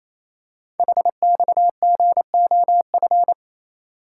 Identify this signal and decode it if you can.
Morse code, heard as a keyed tone, about 27 wpm, 700 Hz. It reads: HXGOF
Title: Code Groups - 5 Letters - 28wpm